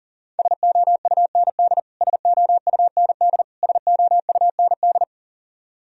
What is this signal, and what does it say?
Morse code, heard as a keyed tone, about 40 wpm, 700 Hz. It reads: SOUND SOUND SOUND
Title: Top 100 Words Repeated 40wpm